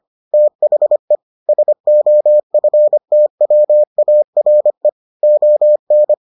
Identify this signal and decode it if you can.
Morse code, heard as a keyed tone, about 25 wpm, 600 Hz. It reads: THE SOFTWARE ON